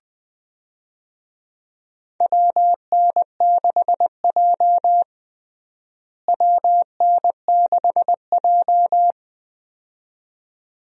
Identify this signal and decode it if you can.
Morse code, heard as a keyed tone, about 20 wpm, 700 Hz. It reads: WN6J WN6J
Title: USA Call Signs - 4 Characters - Repeated 2x - 20wpm